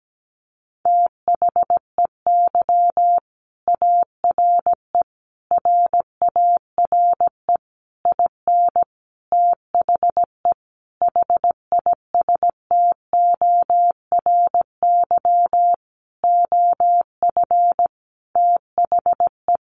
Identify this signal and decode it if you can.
Morse code, heard as a keyed tone, about 17 wpm, 700 Hz. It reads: THEY ARE RARE IN THE HISTORY OF THE
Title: Sentences from Top 1100 Words 17wpm